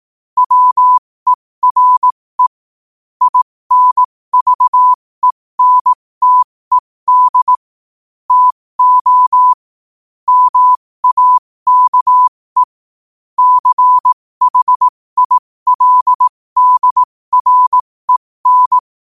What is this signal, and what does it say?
Morse code, heard as a keyed tone, about 18 wpm, spaced as for 15 wpm, 1000 Hz. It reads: WERE INVENTED TO MAKE CHILDREN